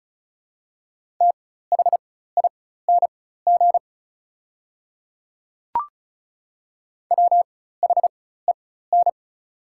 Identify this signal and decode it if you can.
Morse code, heard as a keyed tone, about 36 wpm, spaced as for 16 wpm, 700 Hz. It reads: THING WHEN
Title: 4x Character Spacing - Top 300 Words - 35wpm